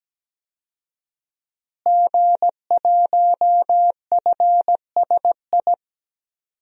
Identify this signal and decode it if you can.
Morse code, heard as a keyed tone, about 17 wpm, 700 Hz. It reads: G1FSI
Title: DXCC-100 Call Signs - 5 Characters - Repeated 2x - 17wpm